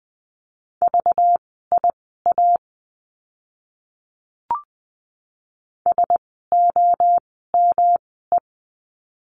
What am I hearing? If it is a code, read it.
Morse code, heard as a keyed tone, about 20 wpm, spaced as for 14 wpm, 700 Hz. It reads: VIA SOME
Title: 192 Lesson 31f - Letter v - Words 20wpm